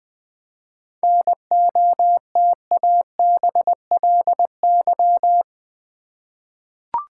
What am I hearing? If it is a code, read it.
Morse code, heard as a keyed tone, about 20 wpm, 700 Hz. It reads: NOTABLY
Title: 7 Letter Words - First 3 Letters Given - 20wpm